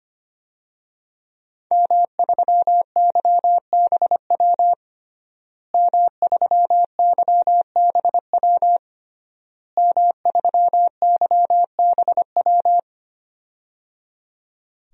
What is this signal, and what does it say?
Morse code, heard as a keyed tone, about 25 wpm, 700 Hz. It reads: M3YBW M3YBW M3YBW
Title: DXCC-100 Call Signs - 5 Characters - Repeated 3x - 25wpm